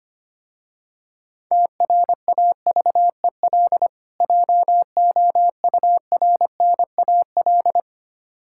Morse code, heard as a keyed tone, about 25 wpm, 700 Hz. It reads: TRAVEL JOURNAL
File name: practice / 25wpm / Difficult Words with Hints - 25wpm